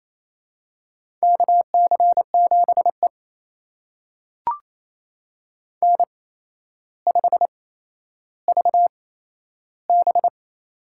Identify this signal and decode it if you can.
Morse code, heard as a keyed tone, about 28 wpm, spaced as for 12 wpm, 700 Hz. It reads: KC7E N5VB
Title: USA Call Signs - 4 Characters - 8x Character Spacing - 28wpm